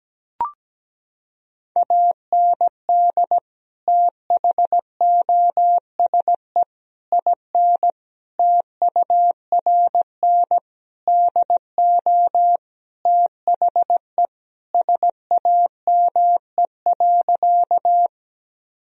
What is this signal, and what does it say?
Morse code, heard as a keyed tone, about 17 wpm, 700 Hz. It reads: AND THOSE IN TURN DO THE SAME.
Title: Sentences from Top 1000 Words - Review Part 1 - 17wpm